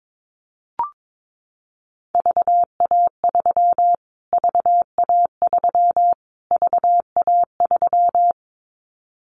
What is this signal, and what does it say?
Morse code, heard as a keyed tone, about 22 wpm, 700 Hz. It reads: VA3 VA3 VA3
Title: Callsign Prefixes for Top 25 DXCC - Repeated 3x - 22wpm